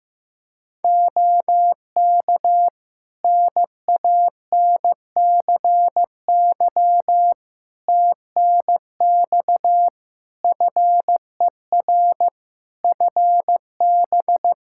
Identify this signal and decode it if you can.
Morse code, heard as a keyed tone, about 15 wpm, 700 Hz. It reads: OK NANCY TNX FER FB
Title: Practice QSO - Exchange 4 - Ending - 15wpm